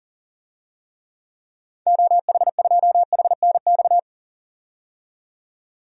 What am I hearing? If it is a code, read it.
Morse code, heard as a keyed tone, about 37 wpm, 700 Hz. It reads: OH2HNX
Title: DXCC-100 Call Signs - 6 Characters - 40wpm